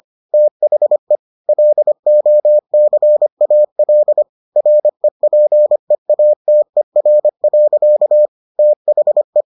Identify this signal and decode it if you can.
Morse code, heard as a keyed tone, about 25 wpm, 600 Hz. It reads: THE LOCAL REPEATER. THE